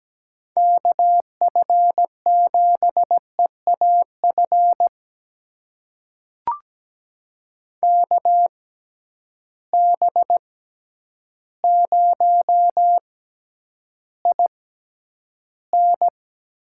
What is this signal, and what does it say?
Morse code, heard as a keyed tone, about 17 wpm, spaced as for 9 wpm, 700 Hz. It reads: KF7EAF KB0IN
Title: USA Call Signs - 6 Characters - 6x Character Spacing - 17wpm